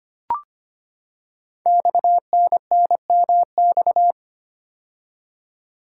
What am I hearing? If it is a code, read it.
Morse code, heard as a keyed tone, about 25 wpm, 700 Hz. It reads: XNNMX
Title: Code Groups - 5 Letters - 25wpm